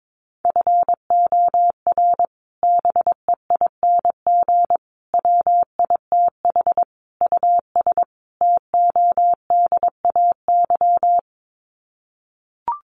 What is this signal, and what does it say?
Morse code, heard as a keyed tone, about 22 wpm, 700 Hz. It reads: FOR BEING WITH US TODAY
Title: Sets of 5 Words Encore2 22wpm